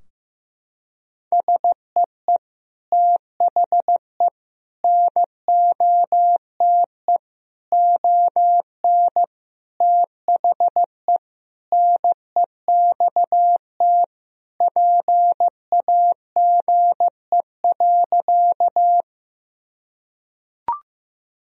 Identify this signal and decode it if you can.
Morse code, heard as a keyed tone, about 15 wpm, 700 Hz. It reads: SEE THE NOTE ON THE NEXT PAGE.
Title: Sentences from Top 1000 Words - Review Part 2 - 15wpm